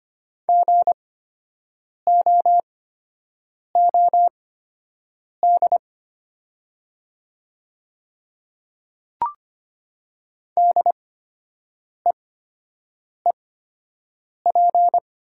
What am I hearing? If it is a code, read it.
Morse code, heard as a keyed tone, about 25 wpm, spaced as for 7 wpm, 700 Hz. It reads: GOOD DEEP
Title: 8x Character Spacing - Top 500 Words - 25wpm